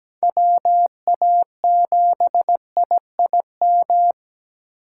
Morse code, heard as a keyed tone, about 20 wpm, 700 Hz. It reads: WA7IIM
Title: Speed Racing - USA 2x3 Call Signs 17wpm